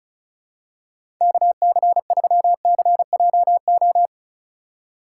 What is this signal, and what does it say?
Morse code, heard as a keyed tone, about 36 wpm, 700 Hz. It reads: KC3CJO
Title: USA Call Signs - 6 Characters - 3x Character Spacing - 35wpm